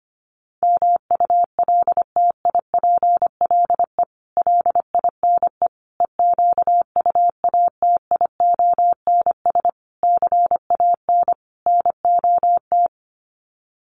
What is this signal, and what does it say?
Morse code, heard as a keyed tone, about 25 wpm, 700 Hz. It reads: MULTIPLE LINE EQUATIONS CAN NOT
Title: Sets of 5 Words 25wpm